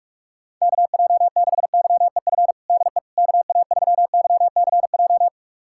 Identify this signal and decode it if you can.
Morse code, heard as a keyed tone, about 47 wpm, 700 Hz. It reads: KJ6YEF DE KA3YCJ
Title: USA Callsign DE Callsign 45wpm